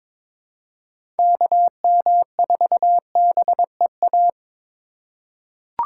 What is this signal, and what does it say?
Morse code, heard as a keyed tone, about 22 wpm, 700 Hz. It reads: KM4BEA
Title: USA Call Signs - 2x Character Spacing - 22wpm